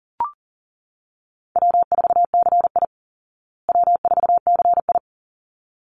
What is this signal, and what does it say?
Morse code, heard as a keyed tone, about 40 wpm, 700 Hz. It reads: W4CI W4CI
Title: Call Signs Repeated 3x - 40wpm